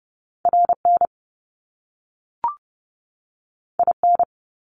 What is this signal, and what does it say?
Morse code, heard as a keyed tone, about 30 wpm, 700 Hz. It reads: RN IN
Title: Top Two Letter Combinations in Words 30wpm